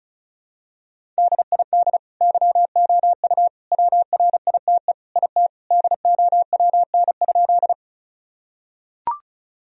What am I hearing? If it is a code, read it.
Morse code, heard as a keyed tone, about 36 wpm, 700 Hz. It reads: DID YOU WRITE IT DOWN?
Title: Sentences from Top 1000 Words - Review Part 1 - 35wpm